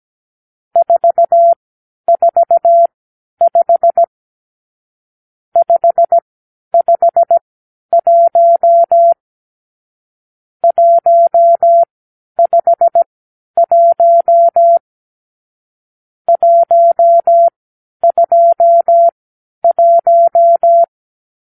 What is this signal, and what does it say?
Morse code, heard as a keyed tone, about 17 wpm, spaced as for 10 wpm, 680 Hz. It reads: 445 551 151 121
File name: arrl / Review (1-5)